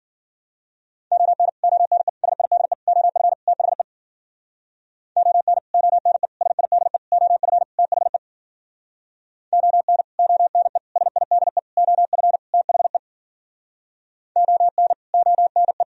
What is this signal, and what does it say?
Morse code, heard as a keyed tone, about 51 wpm, 700 Hz. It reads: ON ONE SIDE OF THE ON ONE SIDE OF THE ON ONE SIDE OF THE ON ONE